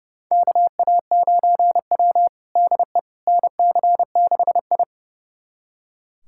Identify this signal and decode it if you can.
Morse code, heard as a keyed tone, about 30 wpm, 700 Hz. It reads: KA9W DE NC6I
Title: USA Callsign DE Callsign - 4 Characters - 30wpm